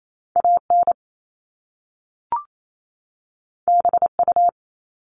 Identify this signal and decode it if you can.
Morse code, heard as a keyed tone, about 28 wpm, 700 Hz. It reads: AN BU
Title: Top Two Letter Combinations in Words 28wpm